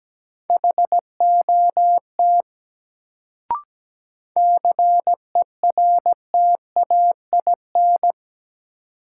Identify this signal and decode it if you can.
Morse code, heard as a keyed tone, about 17 wpm, 700 Hz. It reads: HOT CERTAIN
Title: Sentences from Top 500 Words - New Word Drill - 17wpm